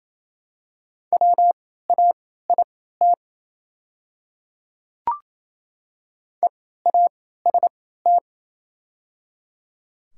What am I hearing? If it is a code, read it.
Morse code, heard as a keyed tone, about 28 wpm, spaced as for 16 wpm, 690 Hz. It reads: WAIT EAST
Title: 3x Character Spacing - Top 500 Words - 28wpm